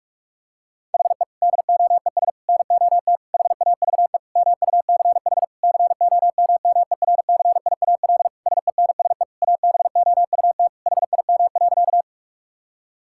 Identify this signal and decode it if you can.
Morse code, heard as a keyed tone, about 46 wpm, 700 Hz. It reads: HE DOES NOT HAVE MUCH COMMERCIAL SENSE ABOUT HIM.
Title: Sentences from Top 1000 Words - Review Part 5 - 45wpm